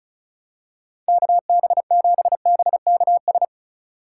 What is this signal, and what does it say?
Morse code, heard as a keyed tone, about 36 wpm, 700 Hz. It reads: KB7BKS
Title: USA Call Signs - 6 Characters - 2x Character Spacing - 35wpm